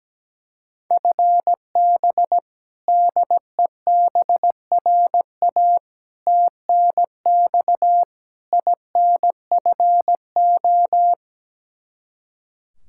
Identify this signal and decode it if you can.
Morse code, heard as a keyed tone, about 17 wpm, 700 Hz. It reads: FB DEBRA TNX INFO
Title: Practice QSO - Exchange 3 - More Info - 17wpm